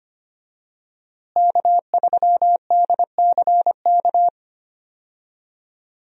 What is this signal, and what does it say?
Morse code, heard as a keyed tone, about 25 wpm, 700 Hz. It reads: K3DCK
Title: USA Call Signs - 5 Characters - Repeated 2x - 25wpm